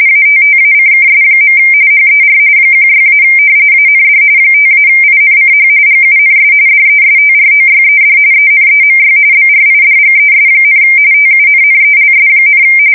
RTTY, about 45.45 baud. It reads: ATTACKS. HER 16- AND 5-INCH GUNS DESTROYED ENEMY BRIDGES, TUNNELS,
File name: fsk170 / battleshipnj